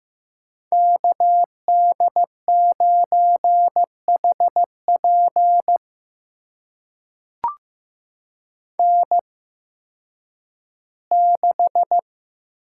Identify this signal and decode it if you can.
Morse code, heard as a keyed tone, about 15 wpm, spaced as for 10 wpm, 700 Hz. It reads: KD9HP N6
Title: USA Call Signs - 5 Characters - 8x Character Spacing - 15wpm